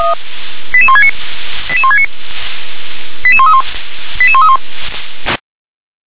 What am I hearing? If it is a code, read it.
Five-tone selective call (ZVEI), heard as quick runs of five tones, tones of about 70 ms. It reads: 88169 88169 88131 88131